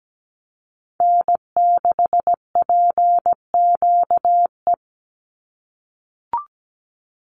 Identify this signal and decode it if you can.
Morse code, heard as a keyed tone, about 17 wpm, 700 Hz. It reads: N6PQE
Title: USA Call Signs 17wpm